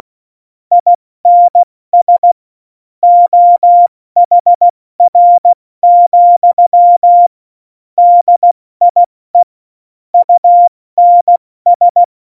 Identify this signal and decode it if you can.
Morse code, heard as a keyed tone, about 16 wpm, 700 Hz. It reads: INS OHR, DIE UNS
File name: Morsekurs / Witze / Witz 27. August 2019, 70 F80